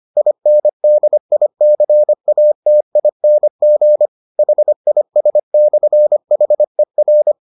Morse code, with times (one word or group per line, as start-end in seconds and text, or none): 0.2-4.1 INDICATING
4.4-7.3 HIS/HER